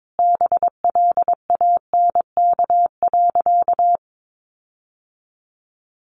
Morse code, heard as a keyed tone, about 22 wpm, 700 Hz. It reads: BLANK.
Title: Sentences from Top 1500 Words 22wpm